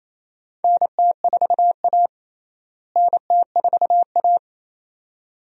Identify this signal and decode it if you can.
Morse code, heard as a keyed tone, about 28 wpm, 700 Hz. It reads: NT4A NT4A
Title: USA Call Signs - 4 Characters - Repeated 2x - 28wpm